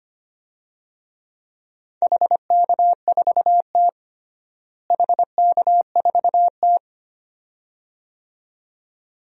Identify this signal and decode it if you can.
Morse code, heard as a keyed tone, about 25 wpm, 700 Hz. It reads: HK4T HK4T
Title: DXCC-100 Call Signs - 4 Characters - Repeated 2x - 25wpm